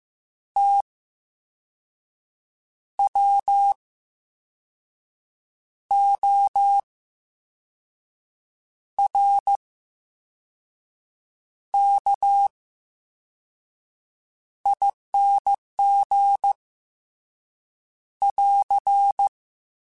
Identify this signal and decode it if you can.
Morse code, heard as a keyed tone, about 15 wpm, spaced as for 5 wpm, 800 Hz. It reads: TWORKING <AR>
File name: Special INGs / Special INGs-mp3s / Slow ing @13